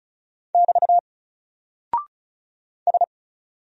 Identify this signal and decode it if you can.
Morse code, heard as a keyed tone, about 36 wpm, 700 Hz. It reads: <BT> S
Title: Single Character (Letter:Number:Punctuation:Prosign) 35wpm